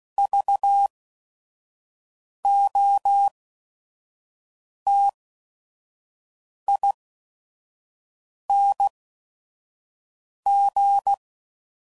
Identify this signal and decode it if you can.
Morse code, heard as a keyed tone, about 16 wpm, spaced as for 5 wpm, 800 Hz. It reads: VOTING
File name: Prefixes / By First Letter / Vs